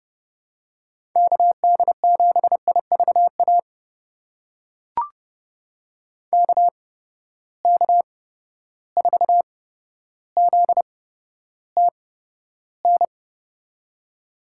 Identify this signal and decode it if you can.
Morse code, heard as a keyed tone, about 30 wpm, spaced as for 13 wpm, 700 Hz. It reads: KD7IVA KK4ZTN
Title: USA Call Signs - 6 Characters - 8x Character Spacing - 30wpm